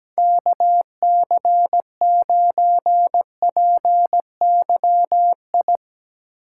Morse code, heard as a keyed tone, about 17 wpm, 700 Hz. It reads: KC9PYI
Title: USA Call Signs - 6 Characters - 3x Character Spacing - 17wpm